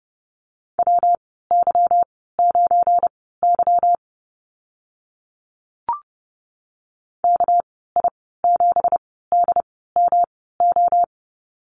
Morse code, heard as a keyed tone, about 30 wpm, spaced as for 17 wpm, 700 Hz. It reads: WY9Y KI7DMO
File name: Lesson 37g - Letter x - Call Signs 30wpm